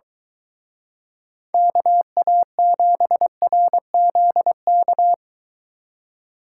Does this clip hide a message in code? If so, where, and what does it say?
Yes: Morse code from 1.5 to 5.1 s, heard as a keyed tone, about 23 wpm, 700 Hz. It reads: KA7RZK